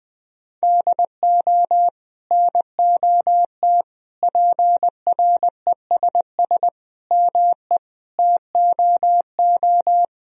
Morse code, heard as a keyed tone, about 20 wpm, 700 Hz. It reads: DO NOT PRESS ME TOO
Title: Sentences from Top 1000 Words - Review Part 1 - 20wpm